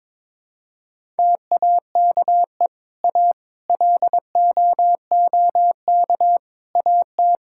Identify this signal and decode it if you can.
Morse code, heard as a keyed tone, about 22 wpm, 700 Hz. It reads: TAKE A LOOK AT
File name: practice / 22wpm / Sentences with 3-Word Phrases - 22wpm